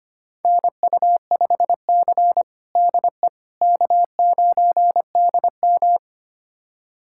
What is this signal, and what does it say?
Morse code, heard as a keyed tone, about 25 wpm, 700 Hz. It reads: NU5C DE K9DM
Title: USA Callsign DE Callsign - 4 Characters - 25wpm